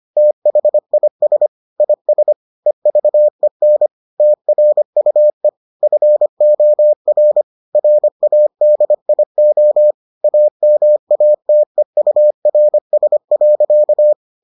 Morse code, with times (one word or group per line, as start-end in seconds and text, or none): 0.2-1.5 THIS
1.8-2.3 IS
2.7-3.9 EVEN
4.2-5.5 TRUE
5.8-7.4 FOR
7.8-9.9 RADIO
10.2-14.1 AMATEURS.